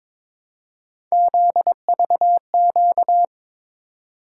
Morse code, heard as a keyed tone, about 22 wpm, 700 Hz. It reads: ZVQ
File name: practice / 22wpm / Three Letter-Number Combinations - 22wpm